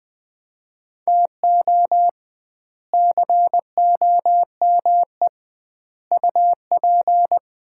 Morse code, heard as a keyed tone, about 21 wpm, 700 Hz. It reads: TO COME UP
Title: 2x Word-Spacing - Sets of 4 Words - 20wpm